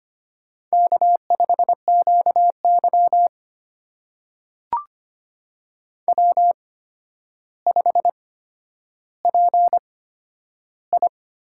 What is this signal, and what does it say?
Morse code, heard as a keyed tone, about 25 wpm, spaced as for 11 wpm, 700 Hz. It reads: K5QY W5PI
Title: USA Call Signs - 4 Characters - 8x Character Spacing - 25wpm